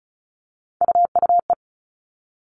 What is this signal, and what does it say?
Morse code, heard as a keyed tone, about 34 wpm, 700 Hz. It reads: UUE
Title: All Cut Numbers - 3-Digit 35wpm